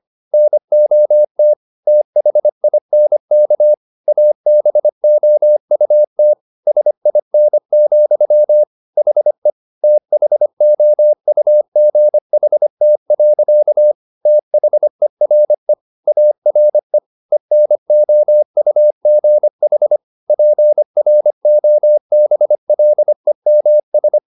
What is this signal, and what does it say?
Morse code, heard as a keyed tone, about 25 wpm, 600 Hz. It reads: NOT THINK ABOUT SIN, HE THOUGHT. THERE ARE ENOUGH PROBLEMS